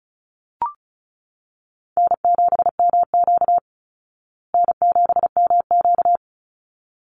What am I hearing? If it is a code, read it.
Morse code, heard as a keyed tone, about 34 wpm, 700 Hz. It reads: N7MQ N7MQ
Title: Easy Call Signs Repeated 2x - 35wpm